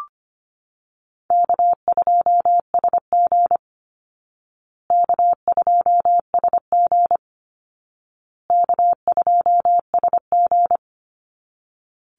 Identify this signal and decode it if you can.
Morse code, heard as a keyed tone, about 25 wpm, 700 Hz. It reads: K2SG K2SG K2SG
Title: Easy Call Signs Repeated 3x - 25wpm